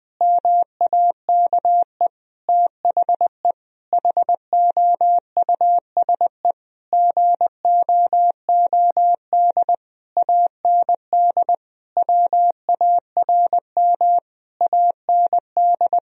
Morse code, heard as a keyed tone, about 20 wpm, 700 Hz. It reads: MAKE THE HOUSE GOOD AND WARM AND
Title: Sentences from Top 1000 Words - Review Part 3 - 20wpm